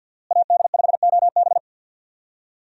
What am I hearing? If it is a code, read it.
Morse code, heard as a keyed tone, about 53 wpm, 700 Hz. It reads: AD5OB
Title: USA Call Signs - 5 Characters - Repeated 2x - 50wpm